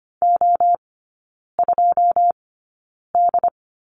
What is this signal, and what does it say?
Morse code, heard as a keyed tone, about 25 wpm, spaced as for 9 wpm, 690 Hz. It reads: O2D
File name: Lesson 14e - Number 2 - Three Characters 25wpm